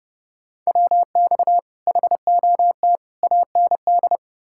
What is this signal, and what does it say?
Morse code, heard as a keyed tone, about 40 wpm, 700 Hz. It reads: WX HOT AND